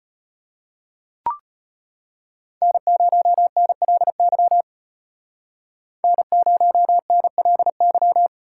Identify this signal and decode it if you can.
Morse code, heard as a keyed tone, about 34 wpm, 700 Hz. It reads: N0NLY N0NLY